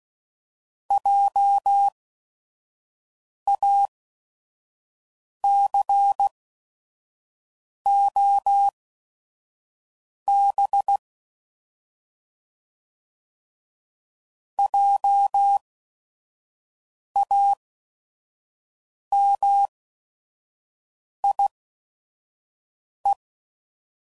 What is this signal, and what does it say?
Morse code, heard as a keyed tone, about 16 wpm, spaced as for 5 wpm, 800 Hz. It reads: JACOB JAMIE